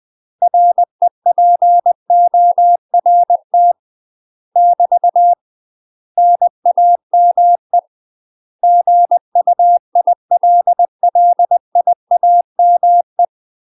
Morse code, heard as a keyed tone, about 20 wpm, 700 Hz. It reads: REPORT <BT> NAME GUILLIAME
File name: UK MASTER QSOs TRAINING / 20WPM / UK MASTER QSOs 0034